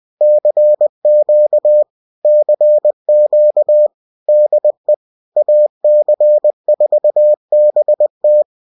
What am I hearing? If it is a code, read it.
Morse code, heard as a keyed tone, about 20 wpm, 600 Hz. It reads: CQ CQ DE AC4BT